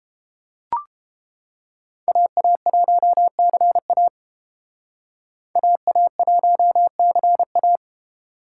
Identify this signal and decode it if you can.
Morse code, heard as a keyed tone, about 28 wpm, 700 Hz. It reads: AA1CA AA1CA